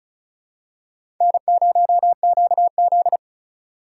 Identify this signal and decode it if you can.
Morse code, heard as a keyed tone, about 34 wpm, 700 Hz. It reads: N0QZ
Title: USA Call Signs - 4 Characters - 3x Character Spacing - 35wpm